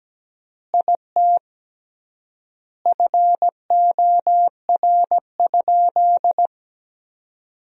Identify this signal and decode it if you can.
Morse code, heard as a keyed tone, about 17 wpm, 700 Hz. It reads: IT FOR?
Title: 3x Word-Spacing - Sentences from Top 100 Words - 17wpm